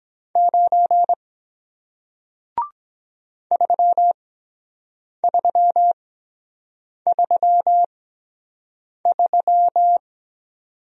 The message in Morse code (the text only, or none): 9 3333